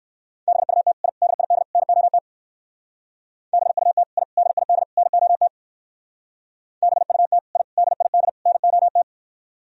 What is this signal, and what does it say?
Morse code, heard as a keyed tone, about 58 wpm, 700 Hz. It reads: BUT I DID NOT BUT I DID NOT BUT I DID NOT